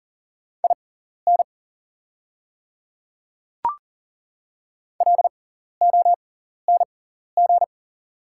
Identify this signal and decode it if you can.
Morse code, heard as a keyed tone, about 40 wpm, spaced as for 14 wpm, 690 Hz. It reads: IN LONG
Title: 6x Character Spacing - Top 100 Words - 40wpm